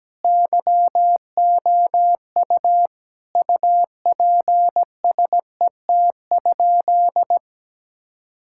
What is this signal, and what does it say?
Morse code, heard as a keyed tone, about 17 wpm, 700 Hz. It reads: YOU UPSET?
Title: Sentences from Top 1000 Words - Review Part 5 - 17wpm